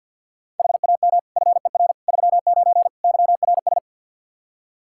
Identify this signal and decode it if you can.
Morse code, heard as a keyed tone, about 47 wpm, 700 Hz. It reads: HAM FER 39 YRS
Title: Practice QSO - Exchange 3 - More Info - 50wpm